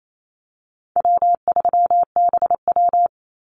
Morse code, heard as a keyed tone, about 28 wpm, 700 Hz. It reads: W3BW
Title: Easy Call Signs - 28wpm